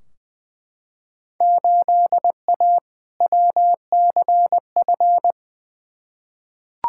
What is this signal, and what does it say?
Morse code, heard as a keyed tone, about 20 wpm, 700 Hz. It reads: 8A WCF
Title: Field Day Exchanges Encore - 20wpm